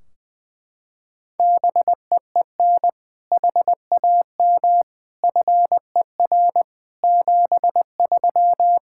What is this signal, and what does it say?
Morse code, heard as a keyed tone, about 20 wpm, 700 Hz. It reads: BEEN HAM FER 73
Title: Practice QSO - Exchange 3 - More Info - 20wpm